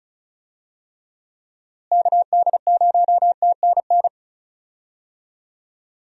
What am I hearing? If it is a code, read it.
Morse code, heard as a keyed tone, about 36 wpm, 700 Hz. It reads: KD0TNN
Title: USA Call Signs - 6 Characters - Repeated 3x - 35wpm